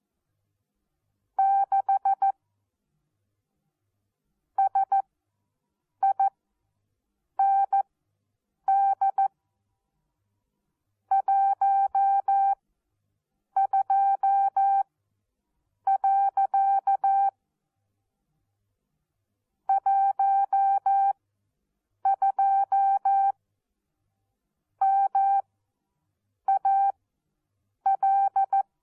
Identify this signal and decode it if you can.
Morse code, heard as a keyed tone, about 14 wpm, spaced as for 7 wpm, 780 Hz. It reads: 6 SIND 12. 12 MAL
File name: Onlinelehrgang / morsen / DJ4UF Morsen14